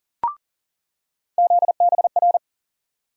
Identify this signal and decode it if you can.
Morse code, heard as a keyed tone, about 40 wpm, 700 Hz. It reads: ZBR
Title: USA Call Sign Postfixes - 3 Letters 40wpm